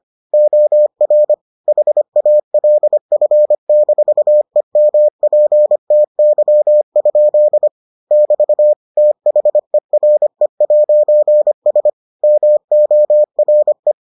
Morse code, heard as a keyed tone, about 25 wpm, 600 Hz. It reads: OR HALF-EMPTY? <BT> THERE'S MORE